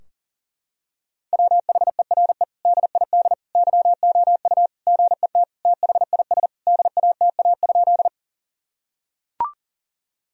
Morse code, heard as a keyed tone, about 37 wpm, 700 Hz. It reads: WHERE DID YOU GET THIS DATA?
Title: Sentences from Top 1000 Words - Review Part 3 - 40wpm